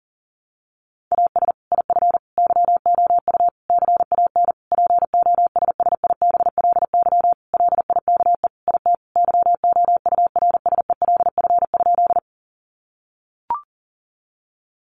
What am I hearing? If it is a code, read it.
Morse code, heard as a keyed tone, about 40 wpm, 700 Hz. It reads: AS IF YOU CAN POSSIBLY LIKE IT YOURSELF?
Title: Sentences from Top 1300 Words 40wpm